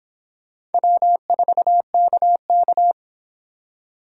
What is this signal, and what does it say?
Morse code, heard as a keyed tone, about 28 wpm, 700 Hz. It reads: W4KK